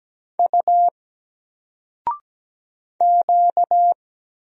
Morse code, heard as a keyed tone, about 17 wpm, 700 Hz. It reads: U Q
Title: Single Letter-Number 17wpm